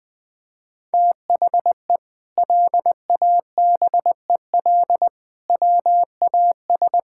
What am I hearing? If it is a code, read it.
Morse code, heard as a keyed tone, about 20 wpm, 700 Hz. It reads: THE LABEL WAS
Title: Sentences with 4-Word Phrases - Spoken Beginning - 20wpm